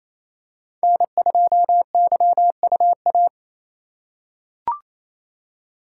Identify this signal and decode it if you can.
Morse code, heard as a keyed tone, about 28 wpm, 700 Hz. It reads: N2YUA
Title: USA Call Signs - 2x Character Spacing - 28wpm